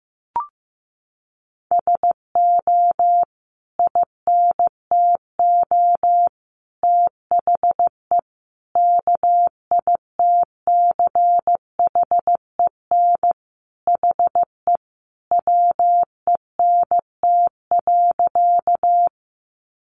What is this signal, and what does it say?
Morse code, heard as a keyed tone, about 15 wpm, 700 Hz. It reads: SO INTO THE KITCHEN HE WENT.
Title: Sentences from Top 1400 Words 15wpm